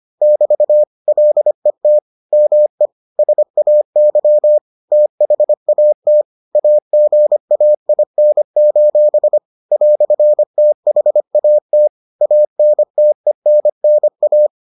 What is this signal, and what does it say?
Morse code, heard as a keyed tone, about 25 wpm, 600 Hz. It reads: <BT> LET ME SAY THAT AGAIN: "THAT ANTENNA